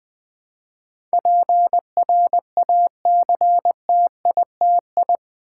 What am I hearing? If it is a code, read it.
Morse code, heard as a keyed tone, about 20 wpm, 700 Hz. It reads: PRACTITI
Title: Build A Word - 4 Letters At A Time - 12-Letter Words - 20wpm